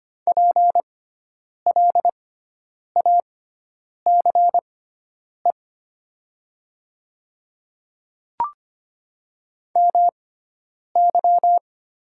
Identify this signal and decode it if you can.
Morse code, heard as a keyed tone, about 25 wpm, spaced as for 9 wpm, 700 Hz. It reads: PLACE MY
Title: 6x Character Spacing - Top 500 Words - 25wpm